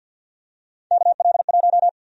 Morse code, heard as a keyed tone, about 50 wpm, 700 Hz. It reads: KL1
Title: USA Call Sign Prefixes - 50wpm